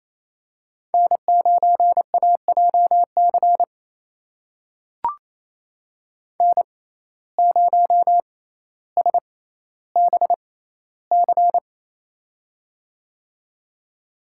Morse code, heard as a keyed tone, about 28 wpm, spaced as for 14 wpm, 700 Hz. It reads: N9AJC N0SBC
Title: USA Call Signs - 5 Characters - 6x Character Spacing - 28wpm